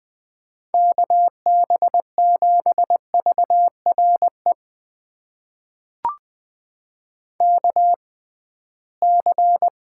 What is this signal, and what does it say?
Morse code, heard as a keyed tone, about 20 wpm, spaced as for 15 wpm, 700 Hz. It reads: KB7VRE KC
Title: USA Call Signs - 6 Characters - 6x Character Spacing - 20wpm